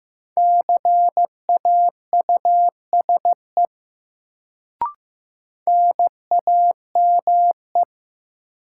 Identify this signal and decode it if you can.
Morse code, heard as a keyed tone, about 15 wpm, 700 Hz. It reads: CAUSE NAME
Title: Sentences from Top 200 Words - New Word Drill - 15wpm